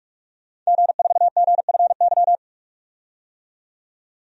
Morse code, heard as a keyed tone, about 47 wpm, 700 Hz. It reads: G4GFY
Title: DXCC-100 Call Signs - 5 Characters - 45wpm